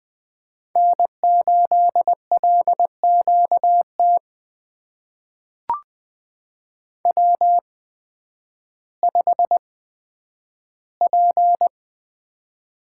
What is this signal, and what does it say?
Morse code, heard as a keyed tone, about 20 wpm, spaced as for 11 wpm, 700 Hz. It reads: N8LQT W5P
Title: USA Call Signs - 8x Character Spacing - 20wpm